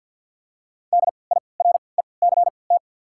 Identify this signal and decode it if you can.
Morse code, heard as a keyed tone, about 53 wpm, spaced as for 29 wpm, 700 Hz. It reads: DIRECT
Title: 3x Character Spacing - Top 500 Words - 50wpm